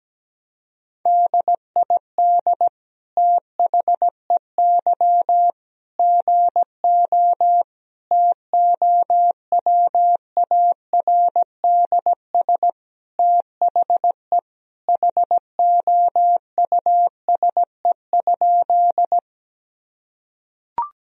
Morse code, heard as a keyed tone, about 17 wpm, 700 Hz. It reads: DID THEY GO TOWARDS THE HOUSE?
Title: Sentences from Top 1000 Words - Review Part 2 - 17wpm